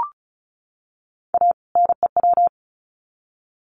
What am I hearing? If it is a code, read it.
Morse code, heard as a keyed tone, about 34 wpm, 700 Hz. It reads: A NEW
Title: Sets of 2 Words 35wpm